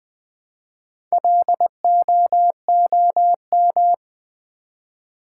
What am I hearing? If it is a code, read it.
Morse code, heard as a keyed tone, about 20 wpm, 700 Hz. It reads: LOOM
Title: Build A Word - 1 Letter At A Time - 3 to 5-Letter Words - 20wpm